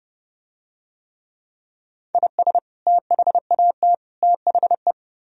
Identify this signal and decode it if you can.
Morse code, heard as a keyed tone, about 29 wpm, 700 Hz. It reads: IS THAT THE